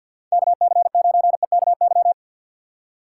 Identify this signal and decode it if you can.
Morse code, heard as a keyed tone, about 53 wpm, 700 Hz. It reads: KC9EKY
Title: USA Call Signs - 6 Characters - 3x Character Spacing - 50wpm